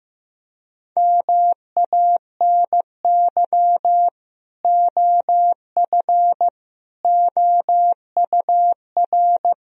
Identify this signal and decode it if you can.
Morse code, heard as a keyed tone, about 15 wpm, 700 Hz. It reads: MANY OF OUR
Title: Sentences from Top 1000 Words - Review Part 3 - 15wpm